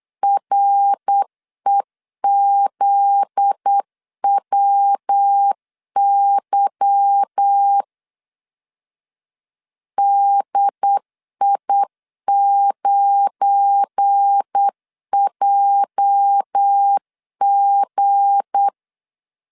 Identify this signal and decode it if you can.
Morse code, heard as a keyed tone, about 8 wpm, 800 Hz. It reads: REZWY DI9JG
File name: CW / DET / 1 9 1500C S C 8wpm 5653